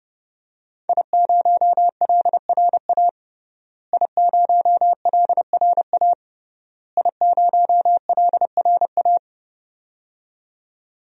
Morse code, heard as a keyed tone, about 30 wpm, 700 Hz. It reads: I0LRA I0LRA I0LRA
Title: DXCC-100 Call Signs - 5 Characters - Repeated 3x - 30wpm